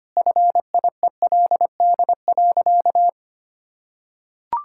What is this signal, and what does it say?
Morse code, heard as a keyed tone, about 25 wpm, 700 Hz. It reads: FIELD.
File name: practice / 25wpm / Sentences with 4-Word Phrases - 25wpm